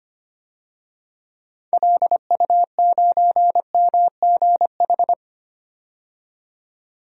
Morse code, heard as a keyed tone, about 25 wpm, 700 Hz. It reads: LU9MGH
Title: DXCC-100 Call Signs - 6 Characters - 25wpm